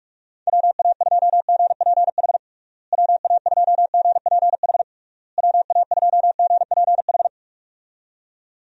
Morse code, heard as a keyed tone, about 46 wpm, 700 Hz. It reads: WA2GPH WA2GPH WA2GPH
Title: USA Call Signs - 6 Characters - Repeated 3x - 45wpm